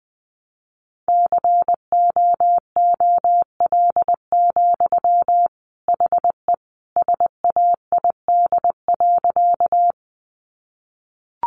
Morse code, heard as a keyed tone, about 20 wpm, 700 Hz. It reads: COOL, HE SAID.
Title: Sentences from Top 1200 Words 20wpm